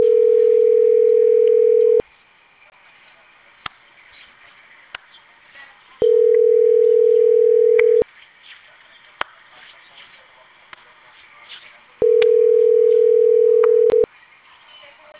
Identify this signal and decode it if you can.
Ringback tone (North America), 440 + 480 Hz.